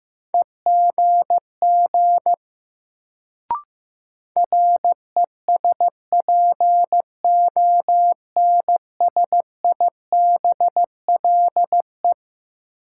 Morse code, heard as a keyed tone, about 15 wpm, 700 Hz. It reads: EGG RESPONSIBLE
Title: Sentences from Top 1000 Words - Review Part 7 - New Word Drill - 15wpm